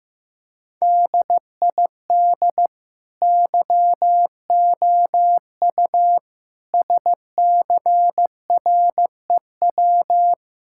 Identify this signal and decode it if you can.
Morse code, heard as a keyed tone, about 15 wpm, 700 Hz. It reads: DID YOU SCREW
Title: Sentences from Top 1600 Words 15wpm